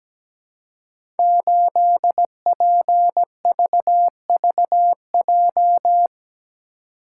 Morse code, heard as a keyed tone, about 17 wpm, 700 Hz. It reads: 8PVVJ
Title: Code Groups - 5 Letter:Numbers - 17wpm